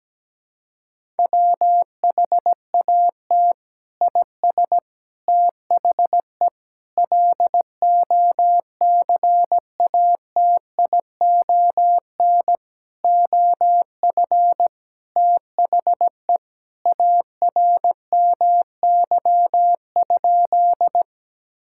Morse code, heard as a keyed tone, about 17 wpm, 700 Hz. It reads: WHAT IS THE LOCATION OF THE ARMY?
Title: Sentences from Top 1000 Words - Review Part 6 - 17wpm